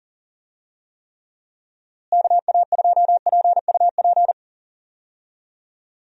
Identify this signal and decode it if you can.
Morse code, heard as a keyed tone, about 40 wpm, 700 Hz. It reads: KA2PUP
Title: USA Call Signs - 6 Characters - Repeated 3x - 40wpm